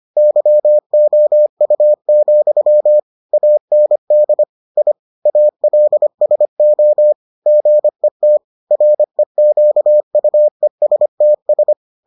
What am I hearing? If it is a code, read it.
Morse code, heard as a keyed tone, about 25 wpm, 600 Hz. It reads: YOU, AND I ALSO GET REQUESTS